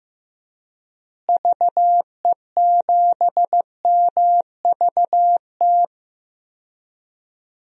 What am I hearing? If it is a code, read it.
Morse code, heard as a keyed tone, about 15 wpm, 700 Hz. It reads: VE7MVT
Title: DXCC-100 Call Signs - 6 Characters - Repeated 2x - 15wpm